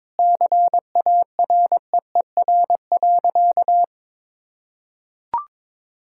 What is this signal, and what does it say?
Morse code, heard as a keyed tone, about 22 wpm, 700 Hz. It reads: CAREER.
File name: Sentences with 7-Word Phrases - 22wpm